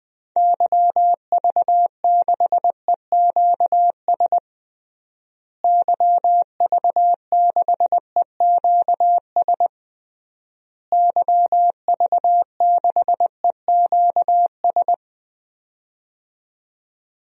Morse code, heard as a keyed tone, about 20 wpm, 700 Hz. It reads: YV6EQS YV6EQS YV6EQS
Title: DXCC-100 Call Signs - 6 Characters - Repeated 3x - 20wpm